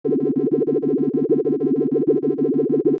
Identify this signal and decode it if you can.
There is no code.